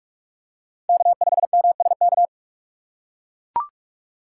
Morse code, heard as a keyed tone, about 43 wpm, 700 Hz. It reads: K5MSK